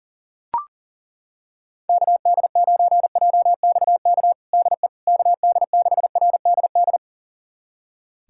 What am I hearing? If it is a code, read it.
Morse code, heard as a keyed tone, about 37 wpm, 700 Hz. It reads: KD9JXK DE KD6RDD
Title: USA Callsign DE Callsign - 6 Characters - 40wpm